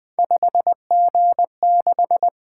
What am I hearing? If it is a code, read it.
Morse code, heard as a keyed tone, about 24 wpm, 700 Hz. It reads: 5G6
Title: Speed-Racing - Three Letter-Number Combinations - 20wpm